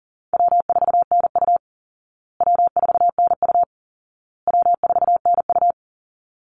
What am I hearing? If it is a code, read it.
Morse code, heard as a keyed tone, about 40 wpm, 700 Hz. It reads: W4NU W4NU W4NU
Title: Easy Call Signs Repeated 3x - 40wpm